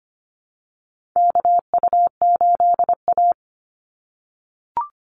Morse code, heard as a keyed tone, about 25 wpm, 700 Hz. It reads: KU8A
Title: USA Call Signs 25wpm